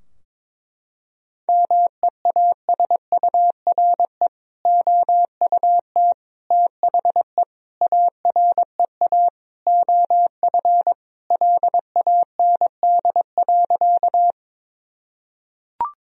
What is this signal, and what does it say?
Morse code, heard as a keyed tone, about 22 wpm, 700 Hz. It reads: MEASURE OUT THE AREA OF LAND.
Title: Sentences from Top 1000 Words - Review Part 2 - 22wpm